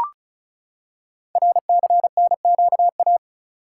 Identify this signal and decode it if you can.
Morse code, heard as a keyed tone, about 36 wpm, 700 Hz. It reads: RCNQA
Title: Code Groups - 5 Letters - 35wpm